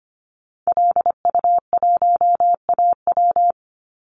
Morse code, heard as a keyed tone, about 25 wpm, 700 Hz. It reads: LU1AW
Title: Call Signs - 25wpm